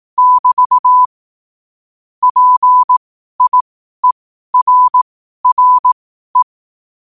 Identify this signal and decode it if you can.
Morse code, heard as a keyed tone, about 18 wpm, spaced as for 12 wpm, 1000 Hz. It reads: <BT> PIERRE
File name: qotdfiles / 12wpm